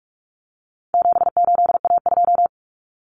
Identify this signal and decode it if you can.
Morse code, heard as a keyed tone, about 44 wpm, 700 Hz. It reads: 78A2
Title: Common Cut Numbers - 4-Digit 45wpm